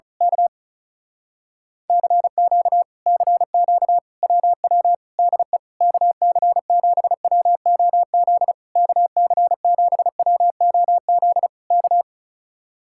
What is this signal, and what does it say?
Morse code, heard as a keyed tone, about 32 wpm, 700 Hz. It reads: K CQ CQ WW DE KC7WOZ KC7WOZ K
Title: Speed-Racing Common QSO Phrases Encore - 35wpm